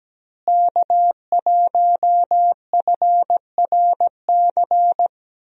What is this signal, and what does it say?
Morse code, heard as a keyed tone, about 17 wpm, 700 Hz. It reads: K1FRC
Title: USA Call Signs - 5 Characters - Repeated 2x - 17wpm